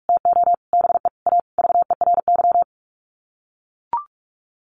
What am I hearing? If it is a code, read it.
Morse code, heard as a keyed tone, about 44 wpm, 700 Hz. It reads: TO BE A VERY
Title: Sets of 5 Words Encore 45wpm